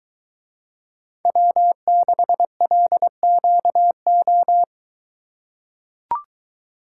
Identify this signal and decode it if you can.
Morse code, heard as a keyed tone, about 24 wpm, 700 Hz. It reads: W6LQO